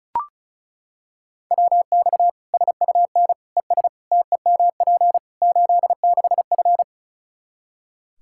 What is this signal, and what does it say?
Morse code, heard as a keyed tone, about 34 wpm, 700 Hz. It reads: WX SUN ES TEMP 86F
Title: K1USN SST QSO Practice 35wpm